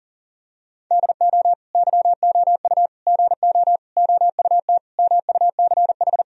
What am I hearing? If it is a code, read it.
Morse code, heard as a keyed tone, about 37 wpm, 700 Hz. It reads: DO YOU GO OUT MUCH
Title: Sentences from Top 1000 Words - Review Part 4 - 40wpm